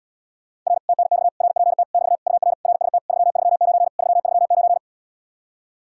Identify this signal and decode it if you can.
Morse code, heard as a keyed tone, about 79 wpm, 700 Hz. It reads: R TNX RPT <BT> UR RST 339 339